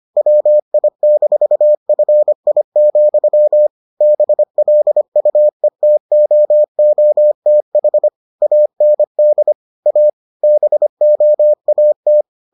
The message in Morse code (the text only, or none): WI-FI, BLUETOOTH AND A BOAT